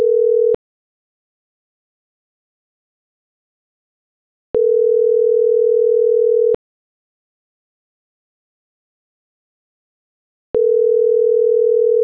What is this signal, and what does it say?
Ringback tone (North America), 440 + 480 Hz.